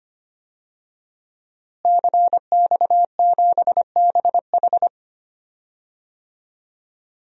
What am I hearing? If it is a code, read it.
Morse code, heard as a keyed tone, about 25 wpm, 700 Hz. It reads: CX7BH
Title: DXCC-100 Call Signs - 5 Characters - Repeated 2x - 25wpm